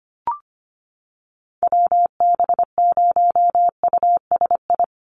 Morse code, heard as a keyed tone, about 25 wpm, 700 Hz. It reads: WB0USI
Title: NA SOTA Chaser Callsigns 25wpm